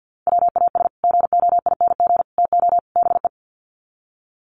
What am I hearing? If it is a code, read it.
Morse code, heard as a keyed tone, about 50 wpm, 700 Hz. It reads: WAS GOING TO BE
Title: Sets of 4 Words 50wpm